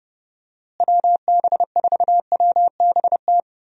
Morse code, heard as a keyed tone, about 30 wpm, 700 Hz. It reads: WB4WBT
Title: USA Call Signs - 6 Characters - Repeated 3x - 30wpm